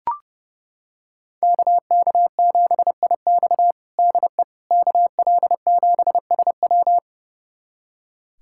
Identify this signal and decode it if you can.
Morse code, heard as a keyed tone, about 30 wpm, 700 Hz. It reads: KK7IX DE KL7SW
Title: USA Callsign DE Callsign - 5 Characters - 30wpm